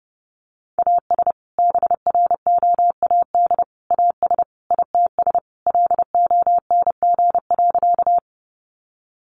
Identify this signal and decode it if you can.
Morse code, heard as a keyed tone, about 28 wpm, 700 Hz. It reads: AS BROAD AS ITS LONG.
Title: Sentences from Top 1000 Words 30wpm